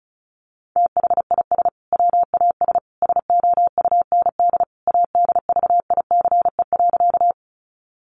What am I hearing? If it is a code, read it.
Morse code, heard as a keyed tone, about 34 wpm, 700 Hz. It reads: THIS WAS SOUND ADVICE.
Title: Sentences from Top 1100 Words 35wpm